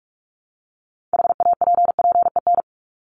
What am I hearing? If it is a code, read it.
Morse code, heard as a keyed tone, about 44 wpm, 700 Hz. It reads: HAPPEN
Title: ED Suffix - Top 100 Words with Base Words 45wpm